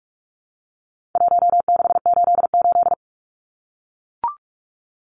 Morse code, heard as a keyed tone, about 44 wpm, 700 Hz. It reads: 1688
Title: Numbers - Contest Style - 4-Digit 45wpm